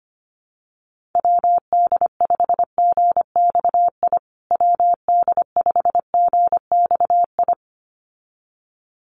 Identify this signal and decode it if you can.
Morse code, heard as a keyed tone, about 25 wpm, 700 Hz. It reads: WD5GXI WD5GXI
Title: USA Call Signs Repeated 2x 25wpm